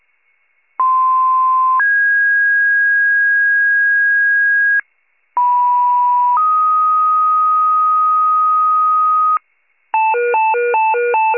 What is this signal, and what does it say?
Two-tone page, heard as a one-second tone, then a three-second tone, about 1040 then 1650 Hz, about 980 then 1250 Hz, with noise. It is heard from 0.8 to 9.4 s.